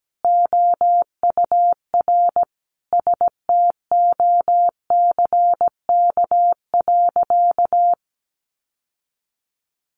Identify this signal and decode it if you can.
Morse code, heard as a keyed tone, about 17 wpm, 700 Hz. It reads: OUR STOCK.
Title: Sentences from Top 1300 Words 17wpm